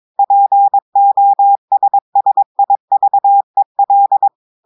Morse code, heard as a keyed tone, about 22 wpm, 800 Hz. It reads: POSSIVEL
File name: audios / portugais / portugais texte en clair 2 vitesse 22 mots minutes